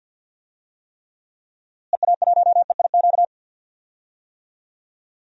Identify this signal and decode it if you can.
Morse code, heard as a keyed tone, about 47 wpm, 700 Hz. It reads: EA1EIQ
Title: DXCC-100 Call Signs - 6 Characters - Repeated 3x - 50wpm